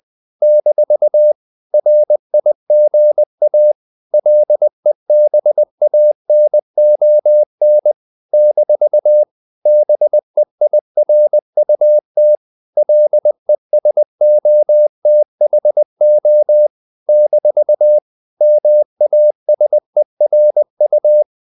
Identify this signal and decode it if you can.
Morse code, heard as a keyed tone, about 20 wpm, 600 Hz. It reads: - RIGA LEBANON - BEIRUT LESOTHO - MASERU
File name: CW / Practice / NationalCapitals20